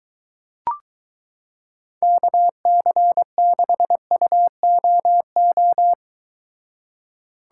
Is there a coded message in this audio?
Yes: Morse code from 2.0 to 5.9 s, heard as a keyed tone, about 23 wpm, 700 Hz. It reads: KC6UOO